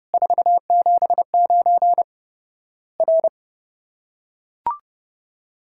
Morse code, heard as a keyed tone, about 30 wpm, 700 Hz. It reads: 479 R
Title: Worked All Europe DX Contest - QTC Exchange - 30wpm